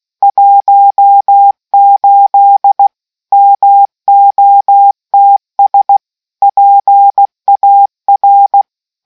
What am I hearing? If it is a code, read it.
Morse code, heard as a keyed tone, about 16 wpm, 800 Hz. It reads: 18 MOTS PAR